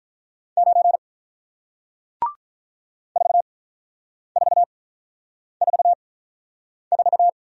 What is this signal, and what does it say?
Morse code, heard as a keyed tone, about 48 wpm, spaced as for 9 wpm, 700 Hz. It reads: 9 4444